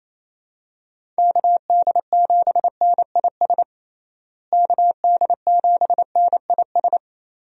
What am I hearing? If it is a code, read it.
Morse code, heard as a keyed tone, about 28 wpm, 700 Hz. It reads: KD7NIS KD7NIS
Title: USA Call Signs - 6 Characters - Repeated 2x - 28wpm